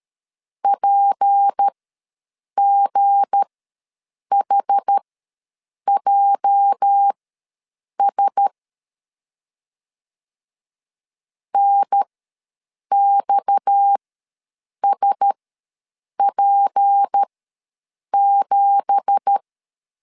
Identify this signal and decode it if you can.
Morse code, heard as a keyed tone, about 13 wpm, spaced as for 7 wpm, 800 Hz. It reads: PGHJS NXSP7
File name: LEZIONI / 9 2867 ws 12wpm 3431